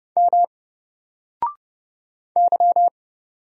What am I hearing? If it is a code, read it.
Morse code, heard as a keyed tone, about 30 wpm, 700 Hz. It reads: M Y
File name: Single-Letter-Number-30wpm